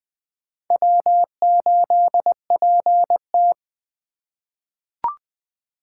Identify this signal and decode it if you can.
Morse code, heard as a keyed tone, about 20 wpm, 700 Hz. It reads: W8PT
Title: USA Call Signs - 4 Characters - 2x Character Spacing - 20wpm